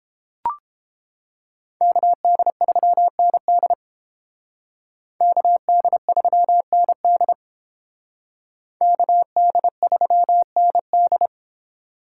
Morse code, heard as a keyed tone, about 30 wpm, 700 Hz. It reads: KD3ND KD3ND KD3ND